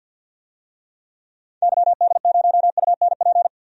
Morse code, heard as a keyed tone, about 50 wpm, 700 Hz. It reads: YD0UNP
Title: DXCC-100 Call Signs - 6 Characters - 50wpm